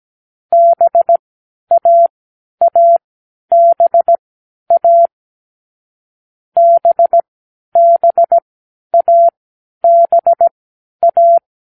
Morse code, heard as a keyed tone, about 17 wpm, spaced as for 10 wpm, 680 Hz. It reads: BAABA BBABA